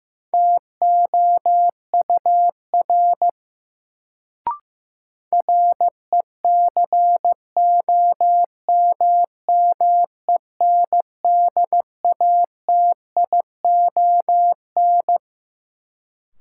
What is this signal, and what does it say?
Morse code, heard as a keyed tone, about 15 wpm, 700 Hz. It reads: TOUR RECOMMENDATION
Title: Sentences from Top 1400 Words - New Word Drill - 15wpm